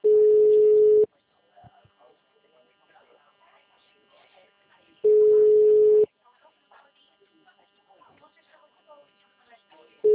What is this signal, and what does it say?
Ringback tone (Europe, 425 Hz), with speech.